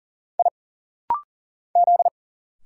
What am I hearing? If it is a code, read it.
Morse code, heard as a keyed tone, about 40 wpm, 700 Hz. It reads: I Z
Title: Single Letter-Number Alt - Rapid-Fire 40wpm